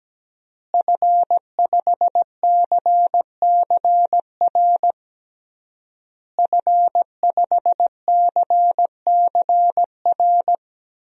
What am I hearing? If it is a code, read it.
Morse code, heard as a keyed tone, about 17 wpm, 700 Hz. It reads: F5CCR F5CCR
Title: DXCC-100 Call Signs - 5 Characters - Repeated 3x - 17wpm